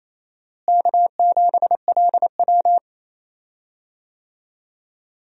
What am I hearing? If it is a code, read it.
Morse code, heard as a keyed tone, about 28 wpm, 700 Hz. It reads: K7LW
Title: USA Call Signs - 4 Characters - Repeated 3x - 28wpm